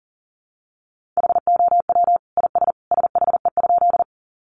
Morse code, heard as a keyed tone, about 40 wpm, 700 Hz. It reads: HOW IS SHE?
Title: Sentences from Top 100 Words 40wpm